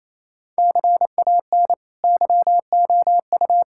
Morse code, heard as a keyed tone, about 28 wpm, 700 Hz. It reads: CAN YOU
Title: Sentences from Top 1000 Words - Review Part 5 - 28wpm